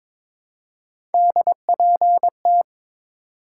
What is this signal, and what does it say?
Morse code, heard as a keyed tone, about 22 wpm, 700 Hz. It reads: DPT
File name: practice / 22wpm / Three Letter-Number Combinations - 22wpm